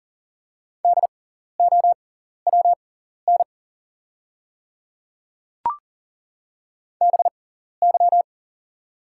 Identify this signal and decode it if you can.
Morse code, heard as a keyed tone, about 38 wpm, spaced as for 14 wpm, 690 Hz. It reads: DOWN BY
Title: 6x Character Spacing - Top 100 Words - 40wpm